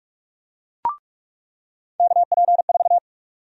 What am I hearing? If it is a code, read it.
Morse code, heard as a keyed tone, about 46 wpm, 700 Hz. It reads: KP4
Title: USA Call Sign Prefixes - 45wpm